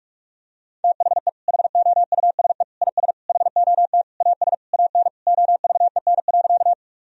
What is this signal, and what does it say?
Morse code, heard as a keyed tone, about 41 wpm, 700 Hz. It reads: THE HOUSE IS HOT AS AN OVEN.
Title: Sentences from Top 1000 Words - Review Part 4 - 45wpm